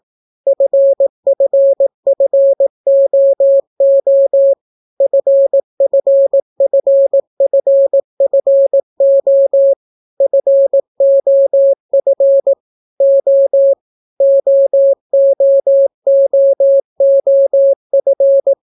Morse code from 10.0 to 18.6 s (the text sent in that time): FOF O OOOOF